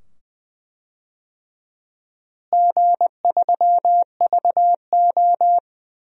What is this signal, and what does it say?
Morse code, heard as a keyed tone, about 20 wpm, 700 Hz. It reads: G3VO
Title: DXCC-100 Call Signs - 4 Characters - Repeated 2x - 20wpm